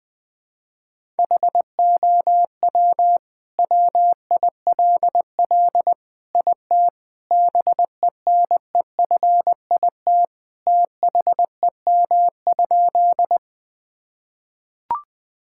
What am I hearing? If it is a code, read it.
Morse code, heard as a keyed tone, about 20 wpm, 700 Hz. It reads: HOW WILL IT BENEFIT THEM?
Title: Sentences from Top 1000 Words - Review Part 6 - 20wpm